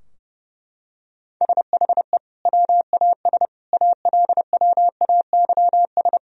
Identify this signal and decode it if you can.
Morse code, heard as a keyed tone, about 30 wpm, 700 Hz. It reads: SHE WAS ALWAYS
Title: Sentences from Top 1000 Words - Review Part 3 - 30wpm